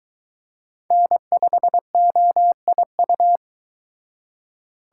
Morse code, heard as a keyed tone, about 22 wpm, 700 Hz. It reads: N5OIU